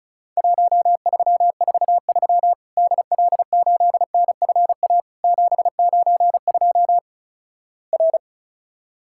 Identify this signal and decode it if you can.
Morse code, heard as a keyed tone, about 36 wpm, 700 Hz. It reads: 1343 DL8NFA 792 R
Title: Worked All Europe DX Contest - QTC Exchange - 35wpm